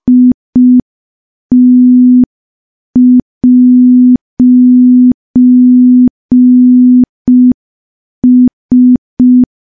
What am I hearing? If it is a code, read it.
Morse code, heard as a keyed tone, about 5 wpm, 250 Hz. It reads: IT'S
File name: morse / morse-training-05WPM